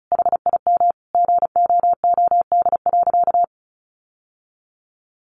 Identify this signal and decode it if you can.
Morse code, heard as a keyed tone, about 34 wpm, 700 Hz. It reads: HIM GOOD.
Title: Sentences from Top 1200 Words 35wpm